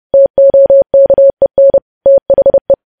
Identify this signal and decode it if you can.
Morse code, heard as a keyed tone, about 30 wpm, 550 Hz. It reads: TOKEN THE
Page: News Headlines in Morse Code at 30 WPM